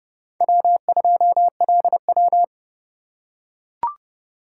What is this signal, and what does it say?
Morse code, heard as a keyed tone, about 29 wpm, 700 Hz. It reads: W2LW
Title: USA Call Signs - 4 Characters - 3x Character Spacing - 30wpm